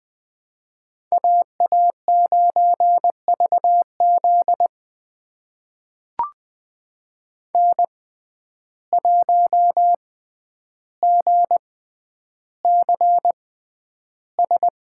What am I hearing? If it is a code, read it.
Morse code, heard as a keyed tone, about 20 wpm, spaced as for 10 wpm, 700 Hz. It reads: AA9VZ N1GCS
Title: USA Call Signs - 6x Character Spacing - 20wpm